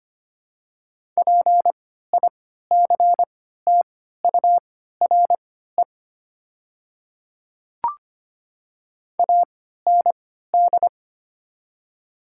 Morse code, heard as a keyed tone, about 25 wpm, spaced as for 14 wpm, 700 Hz. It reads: PICTURE AND
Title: 3x Character Spacing - Top 300 Words - 25wpm